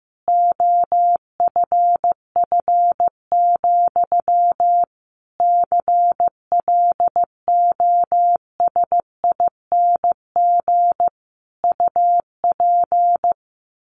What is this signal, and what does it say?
Morse code, heard as a keyed tone, about 15 wpm, 700 Hz. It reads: OFF, CLOSING UP
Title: A Sound Of Thunder 15wpm